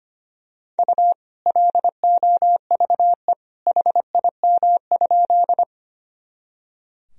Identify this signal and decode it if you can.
Morse code, heard as a keyed tone, about 25 wpm, 700 Hz. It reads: U LOVE HIM?
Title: Sentences from Top 400 Words and Abbreviations 25wpm